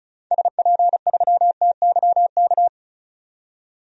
Morse code, heard as a keyed tone, about 36 wpm, 700 Hz. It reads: SP3TYK
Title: DXCC-100 Call Signs - 6 Characters - 35wpm